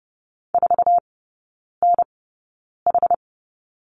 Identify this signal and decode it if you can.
Morse code, heard as a keyed tone, about 30 wpm, spaced as for 9 wpm, 700 Hz. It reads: 4NH
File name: Lesson 16e - Letter c - Three Characters 30wpm